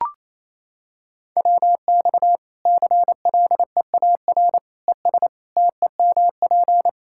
Morse code, heard as a keyed tone, about 28 wpm, 700 Hz. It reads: WX CLEAR ES TEMP
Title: K1USN SST QSO Practice 28wpm